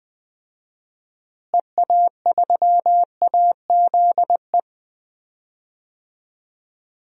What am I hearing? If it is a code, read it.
Morse code, heard as a keyed tone, about 20 wpm, 700 Hz. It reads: EA3AZE
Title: DXCC-100 Call Signs - 6 Characters - Repeated 3x - 20wpm